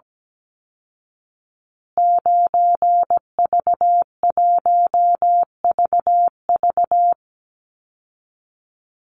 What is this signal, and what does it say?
Morse code, heard as a keyed tone, about 17 wpm, 700 Hz. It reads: 9V1VV
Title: Call Signs Repeated 2x - 17wpm